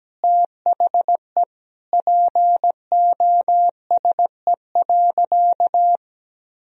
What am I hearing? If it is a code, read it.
Morse code, heard as a keyed tone, about 17 wpm, 700 Hz. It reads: THE POSE.
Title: Sentences from Top 1000 Words - Review Part 2 - 17wpm